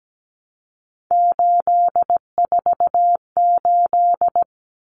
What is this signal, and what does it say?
Morse code, heard as a keyed tone, about 17 wpm, 700 Hz. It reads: 848
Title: Common Cut Numbers - 3-Digit 17wpm